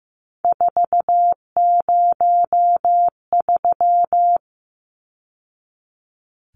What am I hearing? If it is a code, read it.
Morse code, heard as a keyed tone, about 15 wpm, 700 Hz. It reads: 403
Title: Numbers - 3-Digit 15wpm